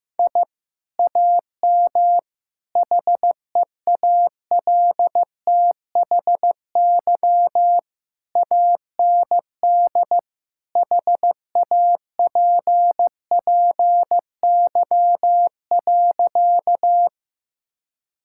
Morse code, heard as a keyed tone, about 16 wpm, 700 Hz. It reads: I AM HEALTHY AND HAPPY.
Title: Sentences from Top 1000 Words - Review Part 5 - 15wpm